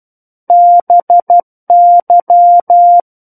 Morse code, heard as a keyed tone, about 12 wpm, 700 Hz. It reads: BY
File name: Book One / 03-Chapter 3 - 12 WPM